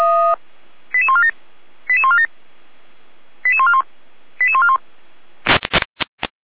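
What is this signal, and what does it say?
Five-tone selective call (ZVEI), heard as quick runs of five tones, tones of about 70 ms. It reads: 88147 88147 88141 88141
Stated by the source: ZVEI five-tone selective call 88149 received on an SDR, from the Feuerwehr Orlamünde server.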